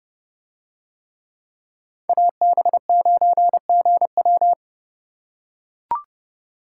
Morse code, heard as a keyed tone, about 29 wpm, 700 Hz. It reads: AB9GW